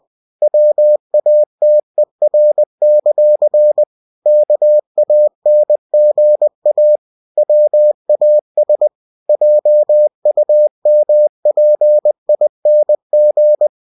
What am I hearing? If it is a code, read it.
Morse code, heard as a keyed tone, about 20 wpm, 600 Hz. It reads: WATER; KANGA WAS JUMPING
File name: winnie-20wpm